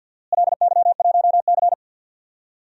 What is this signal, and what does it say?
Morse code, heard as a keyed tone, about 46 wpm, 700 Hz. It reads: LY1C
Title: DXCC-100 Call Signs - 4 Characters - Repeated 3x - 50wpm